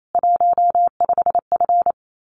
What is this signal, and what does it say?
Morse code, heard as a keyed tone, about 28 wpm, 700 Hz. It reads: 15F
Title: Common QSO Phrases Encore 28wpm